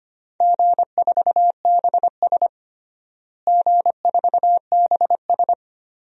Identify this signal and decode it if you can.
Morse code, heard as a keyed tone, about 25 wpm, 700 Hz. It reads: G4BS G4BS
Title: DXCC-100 Call Signs - 4 Characters - Repeated 2x - 25wpm